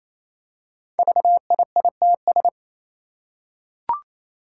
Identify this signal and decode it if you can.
Morse code, heard as a keyed tone, about 29 wpm, 700 Hz. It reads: VIITS
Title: Five Challenging Letter-Number Combinations 28wpm